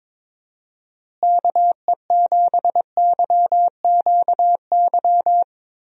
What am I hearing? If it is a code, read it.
Morse code, heard as a keyed tone, about 21 wpm, 700 Hz. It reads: KE7YQY
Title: USA Call Signs - 6 Characters - Repeated 3x - 22wpm